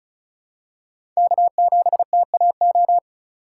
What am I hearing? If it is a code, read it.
Morse code, heard as a keyed tone, about 34 wpm, 690 Hz. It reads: K7TAO
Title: USA Call Signs - 5 Characters - 2x Character Spacing - 35wpm